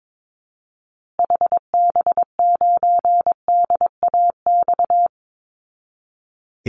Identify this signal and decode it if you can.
Morse code, heard as a keyed tone, about 22 wpm, 700 Hz. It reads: HB9DAX
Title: Call Signs Repeated 2x - 22wpm